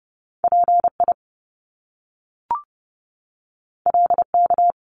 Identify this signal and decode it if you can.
Morse code, heard as a keyed tone, about 30 wpm, 700 Hz. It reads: PI LK
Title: Top Two Letter Combinations in Words 30wpm